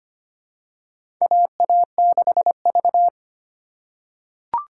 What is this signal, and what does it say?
Morse code, heard as a keyed tone, about 25 wpm, 700 Hz. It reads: AA6V
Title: USA Call Signs - 4 Characters - 2x Character Spacing - 25wpm